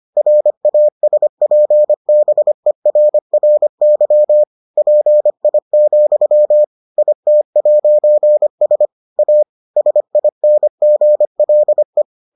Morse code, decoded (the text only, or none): RASPBERRY PI, IT'S A SINGLE